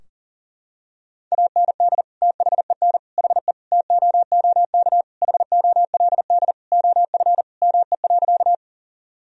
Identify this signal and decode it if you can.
Morse code, heard as a keyed tone, about 37 wpm, 700 Hz. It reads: AND THEN HE TOOK HOLD OF ME.
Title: Sentences from Top 1000 Words - Review Part 2 - 40wpm